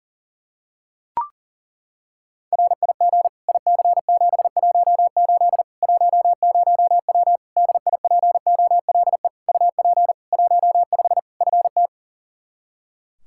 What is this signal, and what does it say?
Morse code, heard as a keyed tone, about 37 wpm, 700 Hz. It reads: RIG IC718 10W DIPOLE UP 15 FT
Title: Practice QSO - Exchange 2 - Info - 40wpm